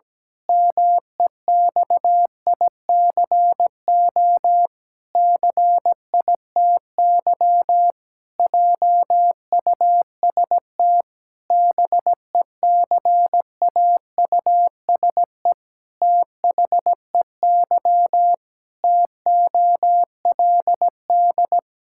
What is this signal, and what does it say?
Morse code, heard as a keyed tone, about 17 wpm, 700 Hz. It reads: MEXICO CITY JUST BECAUSE THEY TOLD
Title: Pangram Challenge v2 - 17wpm